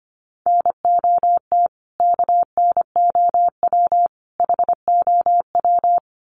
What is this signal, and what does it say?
Morse code, heard as a keyed tone, about 25 wpm, 700 Hz. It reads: NOT KNOW HOW
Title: Sets of 3 Words 25wpm